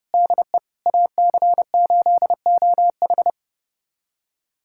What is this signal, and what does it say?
Morse code, heard as a keyed tone, about 29 wpm, 700 Hz. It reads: DE AC8OH
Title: USA Callsign DE Callsign - 5 Characters - 30wpm